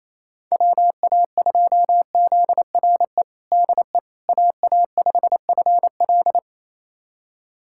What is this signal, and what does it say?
Morse code, heard as a keyed tone, about 28 wpm, 700 Hz. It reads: WA2ZRE DE AA5FL
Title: USA Callsign DE Callsign 28wpm